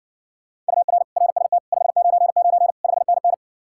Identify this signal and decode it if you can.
Morse code, heard as a keyed tone, about 60 wpm, 700 Hz. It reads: UR RST 599 5NN